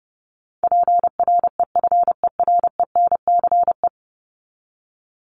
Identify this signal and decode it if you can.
Morse code, heard as a keyed tone, about 29 wpm, 700 Hz. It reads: PREFERENCE
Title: Top 1251-1500 Words 30wpm